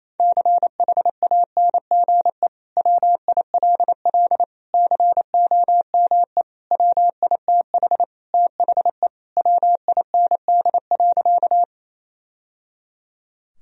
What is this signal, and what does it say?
Morse code, heard as a keyed tone, about 28 wpm, 700 Hz. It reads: CHANGE WILL COME WITH THE WIND.
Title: Sentences from Top 1000 Words - Review Part 2 - 28wpm